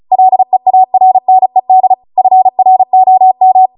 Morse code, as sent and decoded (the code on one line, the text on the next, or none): .-.. .- .-. -. -.. / ..-. .-. --- --
LEARNED FROM